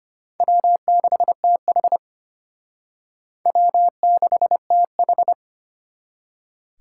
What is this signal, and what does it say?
Morse code, heard as a keyed tone, about 32 wpm, 700 Hz. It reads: W6TH W6TH